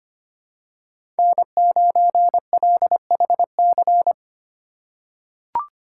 Morse code, heard as a keyed tone, about 25 wpm, 700 Hz. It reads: N9LHC
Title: USA Call Signs - 5 Characters - 4x Character Spacing - 25wpm